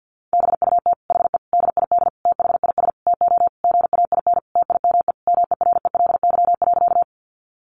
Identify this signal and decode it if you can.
Morse code, heard as a keyed tone, about 50 wpm, 700 Hz. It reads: BUT HE DID THIS TO GAIN TIME MERELY.
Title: Sentences from Top 1400 Words 50wpm